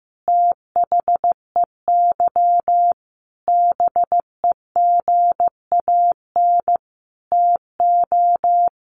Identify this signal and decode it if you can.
Morse code, heard as a keyed tone, about 15 wpm, 700 Hz. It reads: THEY BEGAN TO
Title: Sentences from Top 800 Words 15wpm